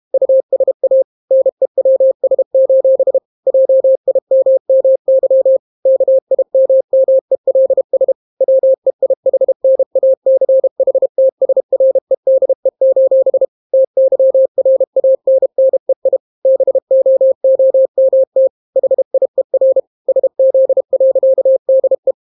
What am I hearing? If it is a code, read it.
Morse code, heard as a keyed tone, about 32 wpm, 530 Hz. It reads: USA NEWS: JIMMY KIMMELS WEIHNACHTSREDE: TYRANNEI BOOMT HIER SZ.DE